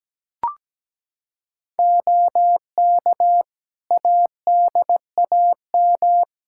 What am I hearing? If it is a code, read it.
Morse code, heard as a keyed tone, about 17 wpm, 700 Hz. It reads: OK ADAM
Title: Practice QSO - Exchange 2 - Info - 17wpm